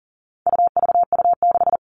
Morse code, heard as a keyed tone, about 40 wpm, 700 Hz. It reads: UVU6
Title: All Cut Numbers - 4-Digit 40wpm